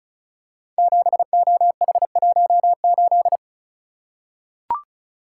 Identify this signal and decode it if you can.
Morse code, heard as a keyed tone, about 36 wpm, 700 Hz. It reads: 7OH18
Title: Code Groups - 5 Letter:Numbers - 35wpm